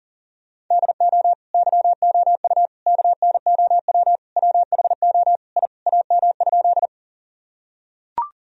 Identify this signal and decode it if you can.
Morse code, heard as a keyed tone, about 40 wpm, 700 Hz. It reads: DO YOU KNOW WHO I AM?
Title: Sentences from Top 1000 Words - Review Part 2 - 40wpm